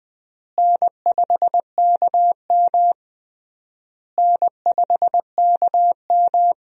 Morse code, heard as a keyed tone, about 20 wpm, 700 Hz. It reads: N5KM N5KM
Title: USA Call Signs - 4 Characters - Repeated 3x - 20wpm